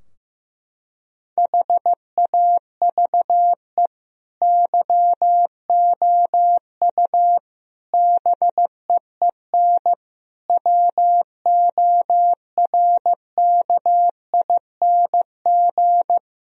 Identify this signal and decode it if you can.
Morse code, heard as a keyed tone, about 15 wpm, 700 Hz. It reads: HAVE YOU BEEN WORKING
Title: Sentences from Top 1000 Words - Review Part 4 - 15wpm